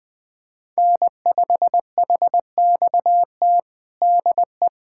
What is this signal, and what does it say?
Morse code, heard as a keyed tone, about 21 wpm, 700 Hz. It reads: N5HXT DE
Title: USA Callsign DE Callsign - 5 Characters - 20wpm